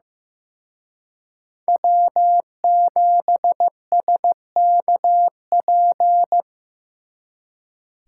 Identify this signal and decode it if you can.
Morse code, heard as a keyed tone, about 18 wpm, 700 Hz. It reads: W7SKP
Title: Speed Racing - USA Call Signs 15wpm